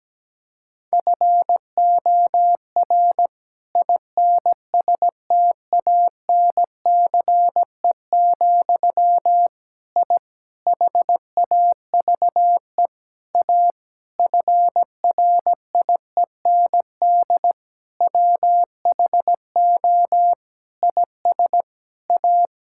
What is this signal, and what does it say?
Morse code, heard as a keyed tone, about 17 wpm, 700 Hz. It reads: FOR INSTANCE, I HAVE A FRIEND WHO IS A
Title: Sentences from Top 1000 Words - Review Part 5 - 17wpm